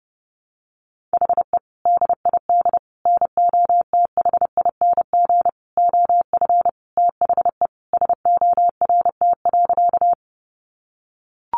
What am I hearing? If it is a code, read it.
Morse code, heard as a keyed tone, about 30 wpm, 700 Hz. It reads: HE DID NOTHING OF THE SORT.
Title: Sentences from Top 1300 Words 30wpm